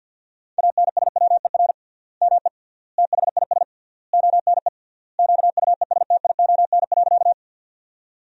Morse code, heard as a keyed tone, about 50 wpm, 700 Hz. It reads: ANSWER ME THIS ONE QUESTION.
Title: 3x Word-Spacing - Sentences from Top 400 Words - 50wpm